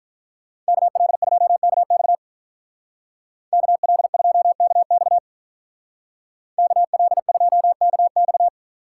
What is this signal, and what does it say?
Morse code, heard as a keyed tone, about 43 wpm, 700 Hz. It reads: KL2KX KL2KX KL2KX
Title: Speed Racing - USA 2x2 Call Signs 35wpm